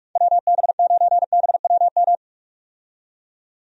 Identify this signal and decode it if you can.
Morse code, heard as a keyed tone, about 46 wpm, 700 Hz. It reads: WB9BWM
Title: USA Call Signs - 6 Characters - 45wpm